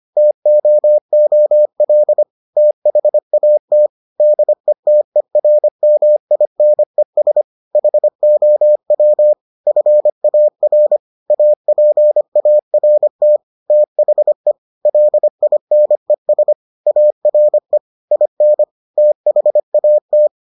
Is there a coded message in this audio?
Yes: Morse code from 0.2 to 20.3 s, heard as a keyed tone, about 25 wpm, 600 Hz. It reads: TOOL THAT DETERMINES HOW FAR APART THE LINES ARE IN THAT